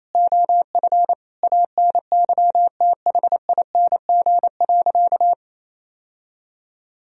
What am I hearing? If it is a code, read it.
Morse code, heard as a keyed tone, about 28 wpm, 700 Hz. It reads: OF ANYTHING.
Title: Sentences from Top 1000 Words - Review Part 3 - 28wpm